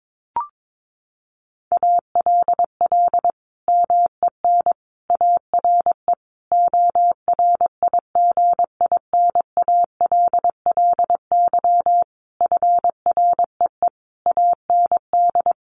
Morse code, heard as a keyed tone, about 22 wpm, 700 Hz. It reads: ALL MEN ARE ORIGINALLY FREE AND
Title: Sentences from Top 1500 Words 22wpm